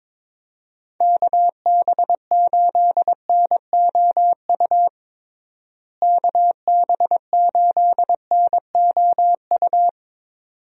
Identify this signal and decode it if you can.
Morse code, heard as a keyed tone, about 22 wpm, 700 Hz. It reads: KB8NOU KB8NOU
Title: USA Call Signs - 6 Characters - Repeated 2x - 22wpm